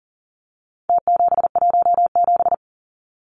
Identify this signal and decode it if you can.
Morse code, heard as a keyed tone, about 40 wpm, 700 Hz. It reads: T717
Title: Numbers - Contest Style - 4-Digit 40wpm